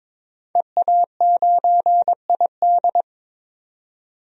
Morse code, heard as a keyed tone, about 22 wpm, 700 Hz. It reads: EA9ID
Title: DXCC-100 Call Signs - 5 Characters - 22wpm